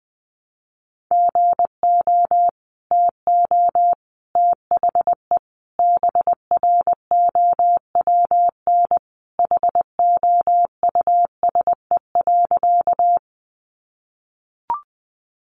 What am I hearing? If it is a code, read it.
Morse code, heard as a keyed tone, about 20 wpm, 700 Hz. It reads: GO TO THE BROWN HOUSE.
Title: Sentences from Top 1200 Words 20wpm